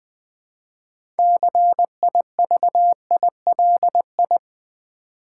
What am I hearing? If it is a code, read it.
Morse code, heard as a keyed tone, about 20 wpm, 700 Hz. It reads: CIVILI
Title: Build A Word - 3 Letters At A Time - 8-Letter Words - 20wpm